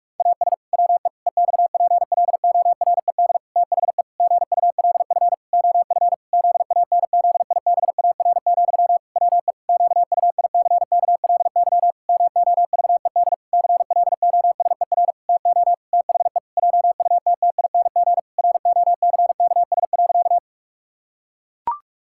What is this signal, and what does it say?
Morse code, heard as a keyed tone, about 46 wpm, 700 Hz. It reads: AS WE EXPLORED THE GULF OF ZANZIBAR, WE QUICKLY MOVED CLOSER TO THE JUTTING ROCKS.
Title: Pangram Challenge v2 - 45wpm